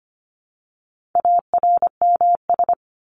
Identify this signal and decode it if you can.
Morse code, heard as a keyed tone, about 25 wpm, 700 Hz. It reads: ARMS
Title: ES:S Suffix - Top 100 Words with Base Words 25wpm